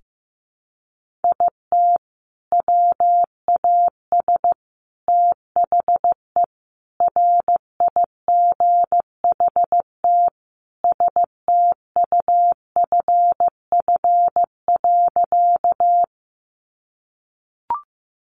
Morse code, heard as a keyed tone, about 15 wpm, 700 Hz. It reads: IT WAS THE RIGHT STUFF.
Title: Sentences from Top 1400 Words 15wpm